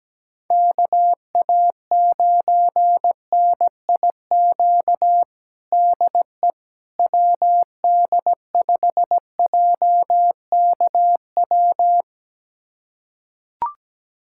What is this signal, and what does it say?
Morse code, heard as a keyed tone, about 17 wpm, 700 Hz. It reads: KA9NIQ DE WD5JKW
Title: USA Callsign DE Callsign - 6 Characters - 17wpm